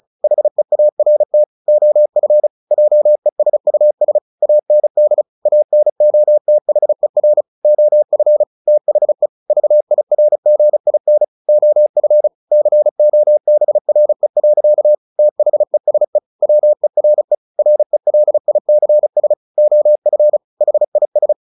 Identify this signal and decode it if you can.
Morse code, heard as a keyed tone, about 35 wpm, 600 Hz. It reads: HEART OF JESUS AND ANOTHER OF THE VIRGIN OF COBRE. THESE WERE RELICS OF HIS
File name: the old man and the sea / the old man and the sea 1 35wpm